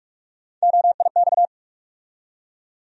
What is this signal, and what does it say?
Morse code, heard as a keyed tone, about 41 wpm, 700 Hz. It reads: OIX
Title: USA Call Sign Postfixes - 3 Letters 45wpm